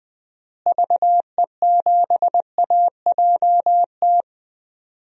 Morse code, heard as a keyed tone, about 20 wpm, 700 Hz. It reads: VE7AJT
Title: DXCC-100 Call Signs - 6 Characters - Repeated 2x - 20wpm